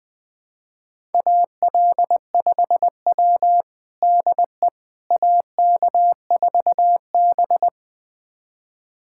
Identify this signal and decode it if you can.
Morse code, heard as a keyed tone, about 20 wpm, 700 Hz. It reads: AL5W DE AK4B
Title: USA Callsign DE Callsign - 4 Characters - 20wpm